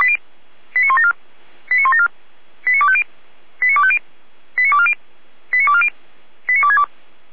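Five-tone selective call (ZVEI), heard as quick runs of five tones, tones of about 70 ms. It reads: 89174 89174 89250 89250 89240 89240 89262
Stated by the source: ZVEI five-tone selective call received with an SDR.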